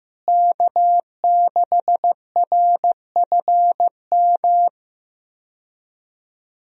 Morse code, heard as a keyed tone, about 15 wpm, 700 Hz. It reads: K6RFM
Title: USA Call Signs - 5 Characters - Repeated 2x - 15wpm